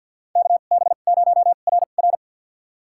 Morse code, heard as a keyed tone, about 52 wpm, spaced as for 37 wpm, 700 Hz. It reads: KB0RR
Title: USA Call Signs - 5 Characters - 2x Character Spacing - 50wpm